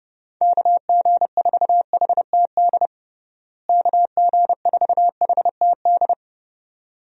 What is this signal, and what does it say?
Morse code, heard as a keyed tone, about 30 wpm, 700 Hz. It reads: KG4HTD KG4HTD
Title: USA Call Signs - 6 Characters - Repeated 2x - 30wpm